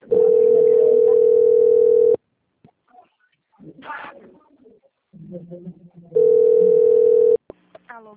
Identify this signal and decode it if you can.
Ringback tone (North America), 440 + 480 Hz, with speech.
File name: out- - - - - .5699